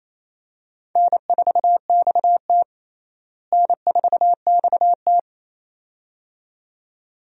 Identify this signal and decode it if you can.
Morse code, heard as a keyed tone, about 28 wpm, 700 Hz. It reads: N4XT N4XT
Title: USA Call Signs - 4 Characters - Repeated 2x - 28wpm